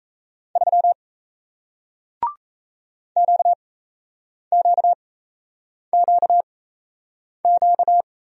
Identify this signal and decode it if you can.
Morse code, heard as a keyed tone, about 38 wpm, spaced as for 8 wpm, 700 Hz. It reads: 3 QQQQ